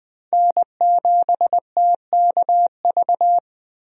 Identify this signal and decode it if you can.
Morse code, heard as a keyed tone, about 20 wpm, 700 Hz. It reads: N7TKV
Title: USA Call Signs - 5 Characters - 20wpm